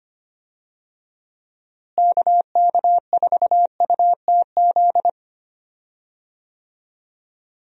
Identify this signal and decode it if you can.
Morse code, heard as a keyed tone, about 25 wpm, 700 Hz. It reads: KK4UTZ
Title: USA Call Signs - 6 Characters - Repeated 2x - 25wpm